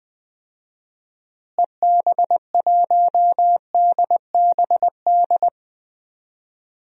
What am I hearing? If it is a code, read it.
Morse code, heard as a keyed tone, about 20 wpm, 700 Hz. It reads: EB1DBD
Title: DXCC-100 Call Signs - 6 Characters - Repeated 3x - 20wpm